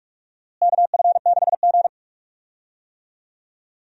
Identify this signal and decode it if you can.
Morse code, heard as a keyed tone, about 46 wpm, 700 Hz. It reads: KF6G
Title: USA Call Signs - 4 Characters - Repeated 3x - 45wpm